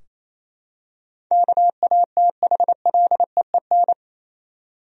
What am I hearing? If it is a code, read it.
Morse code, heard as a keyed tone, about 28 wpm, 700 Hz. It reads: KATHLEEN
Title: USA Common Names - 28wpm